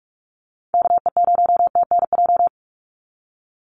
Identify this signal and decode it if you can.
Morse code, heard as a keyed tone, about 44 wpm, 700 Hz. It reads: KE0TNJ
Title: USA Call Signs 45wpm